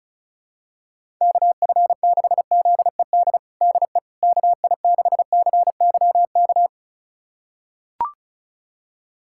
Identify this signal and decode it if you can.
Morse code, heard as a keyed tone, about 36 wpm, 700 Hz. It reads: KF6ZED DE KI6CYK
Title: USA Callsign DE Callsign - 6 Characters - 35wpm